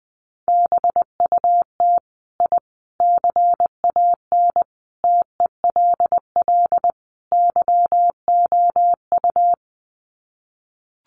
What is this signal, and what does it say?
Morse code, heard as a keyed tone, about 20 wpm, 700 Hz. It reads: BUT I CAN TELL YOU
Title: Sets of 5 Words Encore 20wpm